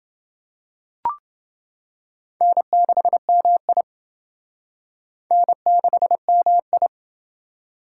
Morse code, heard as a keyed tone, about 28 wpm, 700 Hz. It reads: N6MI N6MI